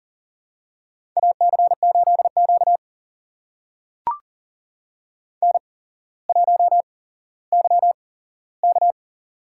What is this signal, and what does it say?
Morse code, heard as a keyed tone, about 38 wpm, spaced as for 17 wpm, 700 Hz. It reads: AC8Q N1YK
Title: USA Call Signs - 8x Character Spacing - 40wpm